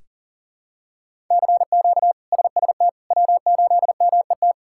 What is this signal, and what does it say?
Morse code, heard as a keyed tone, about 40 wpm, 700 Hz. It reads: CQ SST W8MET
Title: K1USN SST QSO Practice - 40wpm